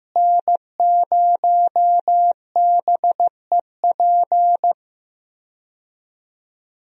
Morse code, heard as a keyed tone, about 15 wpm, 700 Hz. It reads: N0BEP
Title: USA Call Signs - 5 Characters - Repeated 3x - 15wpm